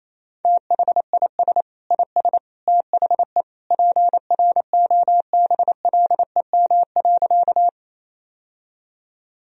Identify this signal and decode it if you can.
Morse code, heard as a keyed tone, about 28 wpm, 700 Hz. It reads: THIS IS THE PROBLEM.
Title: Sentences from Top 1000 Words - Review Part 2 - 28wpm